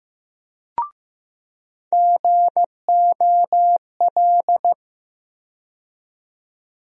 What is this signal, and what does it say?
Morse code, heard as a keyed tone, about 15 wpm, 700 Hz. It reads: GOL
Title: USA Call Sign Postfixes - 3 Letters 15wpm